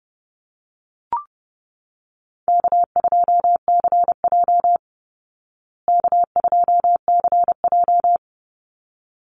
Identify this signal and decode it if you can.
Morse code, heard as a keyed tone, about 30 wpm, 700 Hz. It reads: K2CJ K2CJ
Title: Easy Call Signs Repeated 2x - 30wpm